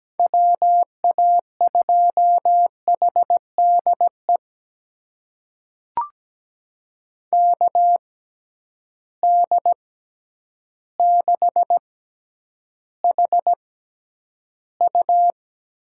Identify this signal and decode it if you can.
Morse code, heard as a keyed tone, about 17 wpm, spaced as for 9 wpm, 700 Hz. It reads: WA2HDE KD6HU
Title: USA Call Signs - 6 Characters - 6x Character Spacing - 17wpm